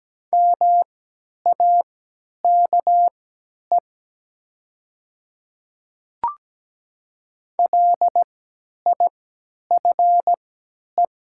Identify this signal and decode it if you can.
Morse code, heard as a keyed tone, about 17 wpm, spaced as for 10 wpm, 700 Hz. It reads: MAKE LIFE
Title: 3x Character Spacing - Top 300 Words - 17wpm